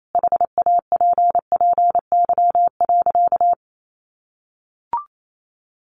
Morse code, heard as a keyed tone, about 28 wpm, 700 Hz. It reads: HAPPY.
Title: Sentences from Top 1400 Words 28wpm